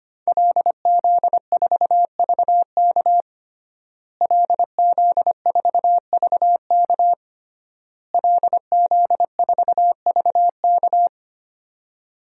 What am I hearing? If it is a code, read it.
Morse code, heard as a keyed tone, about 25 wpm, 700 Hz. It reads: LZ4VK LZ4VK LZ4VK
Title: DXCC-100 Call Signs - 5 Characters - Repeated 3x - 25wpm